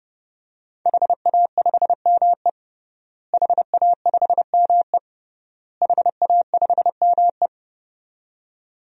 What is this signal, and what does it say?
Morse code, heard as a keyed tone, about 30 wpm, 700 Hz. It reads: HA5ME HA5ME HA5ME
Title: DXCC-100 Call Signs - 5 Characters - Repeated 3x - 30wpm